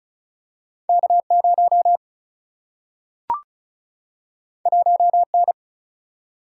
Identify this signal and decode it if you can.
Morse code, heard as a keyed tone, about 36 wpm, 700 Hz. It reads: K 0 1 N
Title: Single-Letter Single-Number 35wpm